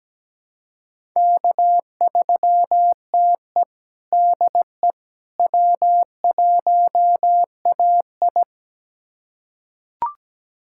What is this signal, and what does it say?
Morse code, heard as a keyed tone, about 17 wpm, 700 Hz. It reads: K3TE DE W1AI
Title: USA Callsign DE Callsign - 4 Characters - 17wpm